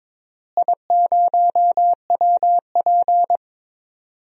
Morse code, heard as a keyed tone, about 22 wpm, 700 Hz. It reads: I0WP
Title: DXCC-100 Call Signs - 4 Characters - Repeated 3x - 22wpm